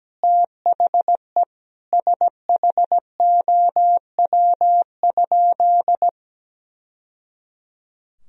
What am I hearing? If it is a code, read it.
Morse code, heard as a keyed tone, about 17 wpm, 700 Hz. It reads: THE SHOW?
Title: Sentences from Top 1000 Words - Review Part 2 - 17wpm